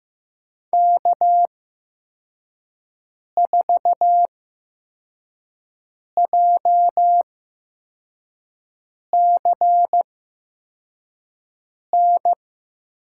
Morse code, heard as a keyed tone, about 15 wpm, spaced as for 4 wpm, 700 Hz. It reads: K4JCN
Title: USA Call Signs - 5 Characters - 8x Character Spacing - 15wpm